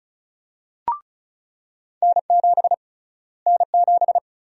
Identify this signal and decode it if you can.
Morse code, heard as a keyed tone, about 34 wpm, 700 Hz. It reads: N7 N7
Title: USA Call Sign Prefixes - Repeated 2x - 35wpm